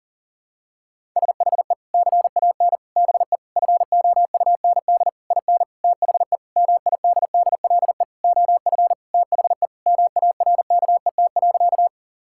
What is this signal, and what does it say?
Morse code, heard as a keyed tone, about 40 wpm, 700 Hz. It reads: SHE CAN BE FOUND IN THE MIDDLE OF THE MARKET.
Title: Sentences from Top 1000 Words - Review Part 6 - 40wpm